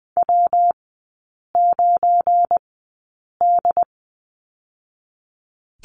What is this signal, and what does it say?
Morse code, heard as a keyed tone, about 20 wpm, spaced as for 8 wpm, 690 Hz. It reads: W9D
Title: 212 Lesson 34e - Number 8 - Three Characters 20wpm